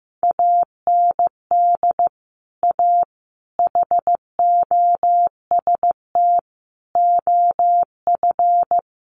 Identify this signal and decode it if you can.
Morse code, heard as a keyed tone, about 14 wpm, 700 Hz. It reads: AND A HOST OF
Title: Sets of 5 Words Encore2 15wpm